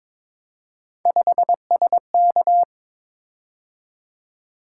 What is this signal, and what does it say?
Morse code, heard as a keyed tone, about 22 wpm, 700 Hz. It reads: 5SK
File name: Three Letter-Number Combinations - 22wpm